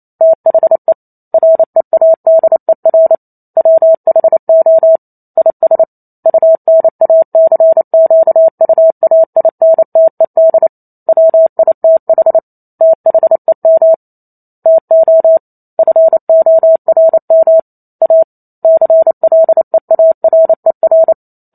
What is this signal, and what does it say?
Morse code, heard as a keyed tone, about 28 wpm, 650 Hz. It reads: THE READER WHO IS UNACQUAINTED WITH THEM TO FORM A CLEARER